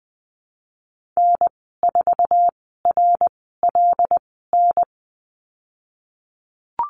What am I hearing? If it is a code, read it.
Morse code, heard as a keyed tone, about 20 wpm, spaced as for 14 wpm, 700 Hz. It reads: N4RLN
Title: 128 Lesson 21g - Number 6 - Call Signs 20wpm